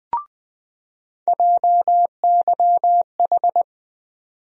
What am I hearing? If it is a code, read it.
Morse code, heard as a keyed tone, about 20 wpm, 700 Hz. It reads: JYH
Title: USA Call Sign Postfixes - 3 Letters 20wpm